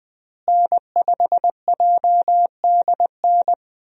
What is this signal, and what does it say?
Morse code, heard as a keyed tone, about 20 wpm, 700 Hz. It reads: N5JDN
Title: USA Call Signs - 5 Characters - Repeated 3x - 20wpm